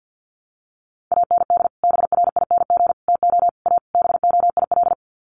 Morse code, heard as a keyed tone, about 50 wpm, 700 Hz. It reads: AND BRING TO A BOIL
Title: Sets of 5 Words Encore 50wpm